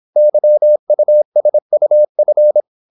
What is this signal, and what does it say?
Morse code, heard as a keyed tone, about 26 wpm, 600 Hz. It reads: YUSUF
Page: Never 26wpm | CW med Gnister